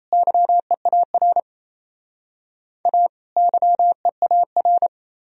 Morse code, heard as a keyed tone, about 34 wpm, 700 Hz. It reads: YEAR A YEAR
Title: Speed-Racing - Sets of 2 Words 28wpm